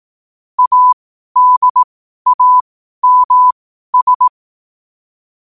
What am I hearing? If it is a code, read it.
Morse code, heard as a keyed tone, about 18 wpm, spaced as for 12 wpm, 1000 Hz. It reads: ADAMS
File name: qotdfiles / 12wpm